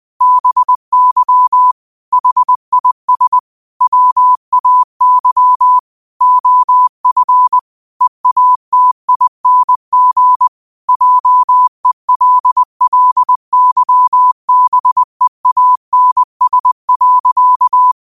Morse code, heard as a keyed tone, about 20 wpm, 1000 Hz. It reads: BY HIS WAY OF EATING JELLYBEANS.
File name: qotdfiles / 20wpm